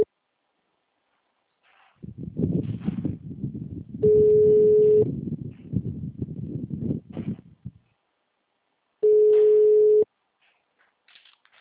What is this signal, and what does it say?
Ringback tone (Europe, 425 Hz).
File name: out- -2003- - - .132